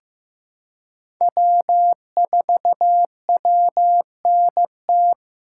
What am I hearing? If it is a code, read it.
Morse code, heard as a keyed tone, about 16 wpm, 700 Hz. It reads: W4WNT
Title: K1USN SST QSO Practice - 15wpm